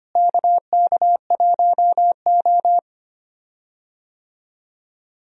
Morse code, heard as a keyed tone, about 25 wpm, 700 Hz. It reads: KK1O
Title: USA Call Signs - 4 Characters - Repeated 3x - 25wpm